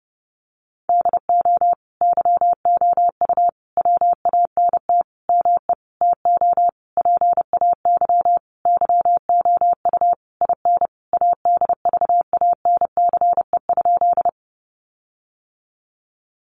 Morse code, heard as a keyed tone, about 30 wpm, 700 Hz. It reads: DO YOU WANT ME TO PAY YOU IN ADVANCE?
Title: Sentences from Top 1400 Words 30wpm